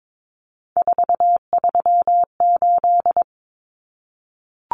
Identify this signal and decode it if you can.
Morse code, heard as a keyed tone, about 22 wpm, 700 Hz. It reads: 438
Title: Common Cut Numbers - 3-Digit 22wpm